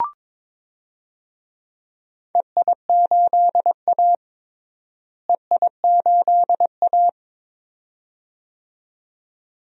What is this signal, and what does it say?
Morse code, heard as a keyed tone, about 22 wpm, 700 Hz. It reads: EI8A EI8A
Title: DXCC-100 Call Signs - 4 Characters - Repeated 2x - 22wpm